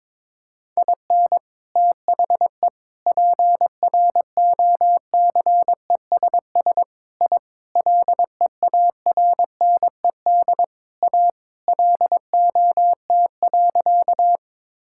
Morse code, heard as a keyed tone, about 22 wpm, 700 Hz. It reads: IN THE PROCESS I LEARNED A LOT.
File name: practice / 22wpm / Sentences with 3-Word Phrases - 22wpm